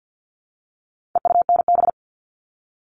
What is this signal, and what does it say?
Morse code, heard as a keyed tone, about 50 wpm, 700 Hz. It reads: EUNB
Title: All Cut Numbers - 4-Digit 50wpm